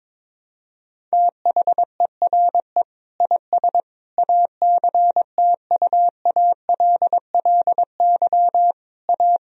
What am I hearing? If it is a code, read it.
Morse code, heard as a keyed tone, about 22 wpm, 700 Hz. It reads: THERE IS ACTUALLY A
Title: Sentences from Top 1000 Words - Review Part 4 - 22wpm